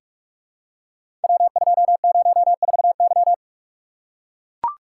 Morse code, heard as a keyed tone, about 44 wpm, 700 Hz. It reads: W204Y
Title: Code Groups - 5 Letter:Numbers - 45wpm